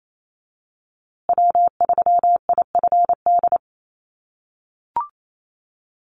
Morse code, heard as a keyed tone, about 28 wpm, 700 Hz. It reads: W3IFD
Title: USA Call Signs Repeated 2x 28wpm